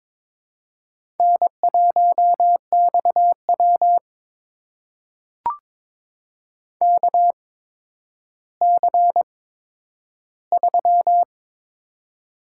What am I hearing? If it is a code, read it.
Morse code, heard as a keyed tone, about 22 wpm, spaced as for 11 wpm, 700 Hz. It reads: N1XW KC3
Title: USA Call Signs - 8x Character Spacing - 22wpm